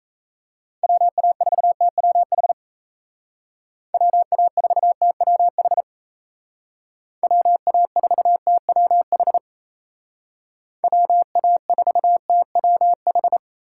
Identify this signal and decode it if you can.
Morse code, heard as a keyed tone, about 34 wpm, 700 Hz. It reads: WA4TWH WA4TWH WA4TWH WA4TWH
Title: Speed Racing - USA Call Signs 28wpm